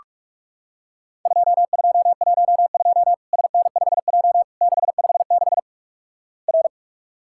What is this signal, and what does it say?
Morse code, heard as a keyed tone, about 48 wpm, 700 Hz. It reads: 2212 SN5J 656 R
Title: Worked All Europe DX Contest - QTC Exchange - 45wpm